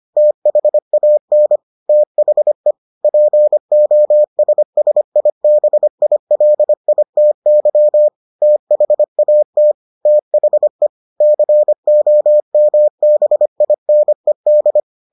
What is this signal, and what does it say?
Morse code, heard as a keyed tone, about 25 wpm, 600 Hz. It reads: THAN THE POSSIBILITY THAT THE COMBINED